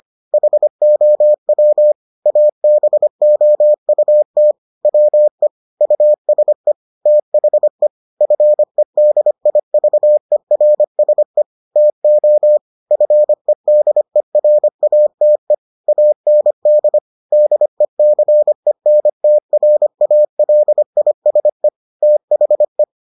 Morse code, heard as a keyed tone, about 25 wpm, 600 Hz. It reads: HOW ABOUT WE USE THE FEDIVERSE TO FEDERATE AND DECENTRALISE THE